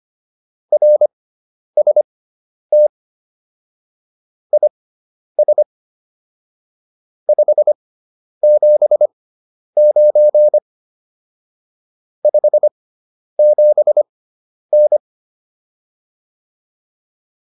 Morse code, heard as a keyed tone, about 25 wpm, spaced as for 10 wpm, 600 Hz. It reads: RST IS 579 57N
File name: QSO102 10